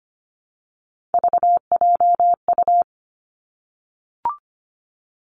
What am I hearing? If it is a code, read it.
Morse code, heard as a keyed tone, about 25 wpm, 700 Hz. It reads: VJU
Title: Three Challenging Letter-Number Combinations 25wpm